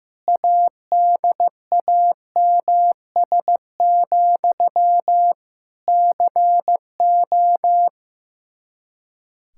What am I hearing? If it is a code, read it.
Morse code, heard as a keyed tone, about 15 wpm, 700 Hz. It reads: ADAMS, CO
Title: Practice QSO - Exchange 1 - Beginning - 15wpm